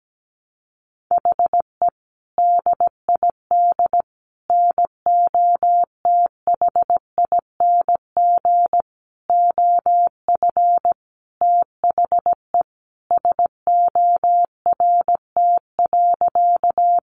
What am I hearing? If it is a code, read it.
Morse code, heard as a keyed tone, about 17 wpm, 700 Hz. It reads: HE DID NOTHING OF THE SORT.
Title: Sentences from Top 1300 Words 17wpm